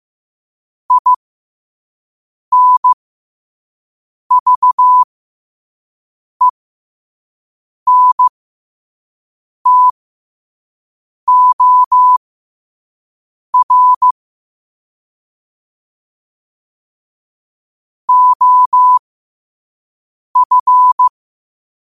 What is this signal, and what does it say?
Morse code, heard as a keyed tone, about 15 wpm, spaced as for 5 wpm, 1000 Hz. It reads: INVENTOR OF
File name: 5wpm